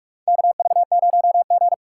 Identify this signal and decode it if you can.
Morse code, heard as a keyed tone, about 46 wpm, 700 Hz. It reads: KV0G
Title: USA Call Signs - 4 Characters - 4x Character Spacing - 45wpm